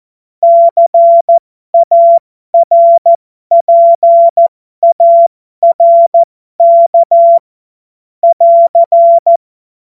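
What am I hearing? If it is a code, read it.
Morse code, heard as a keyed tone, about 14 wpm, spaced as for 12 wpm, 680 Hz. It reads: CARPARK <AR>
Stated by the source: Morse code joke practice.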